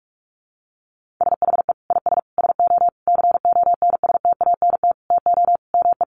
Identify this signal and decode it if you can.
Morse code, heard as a keyed tone, about 44 wpm, 700 Hz. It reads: SHE IS SO CONSTANT TO ME
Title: Sentences from Top 1500 Words 45wpm